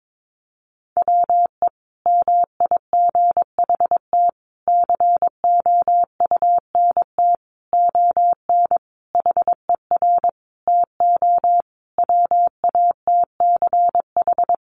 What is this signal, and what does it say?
Morse code, heard as a keyed tone, about 22 wpm, 700 Hz. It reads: WE MIGHT COUNT ON HER TO WATCH
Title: Sentences from Top 1500 Words 22wpm